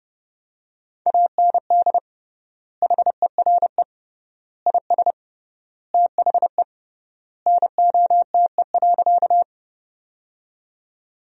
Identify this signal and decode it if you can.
Morse code, heard as a keyed tone, about 30 wpm, 700 Hz. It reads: AND HERE IS THE NOTE.
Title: 3x Word-Spacing - Sentences from Top 500 Words - 30wpm